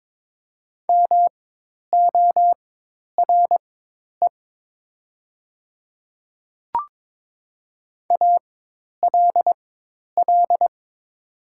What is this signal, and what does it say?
Morse code, heard as a keyed tone, about 22 wpm, spaced as for 10 wpm, 700 Hz. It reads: MORE ALL
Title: 4x Character Spacing - Top 100 Words - 22wpm